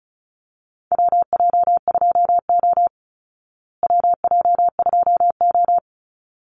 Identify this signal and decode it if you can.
Morse code, heard as a keyed tone, about 34 wpm, 700 Hz. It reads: WJ2O WJ2O
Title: Call Signs Repeated 3x - 35wpm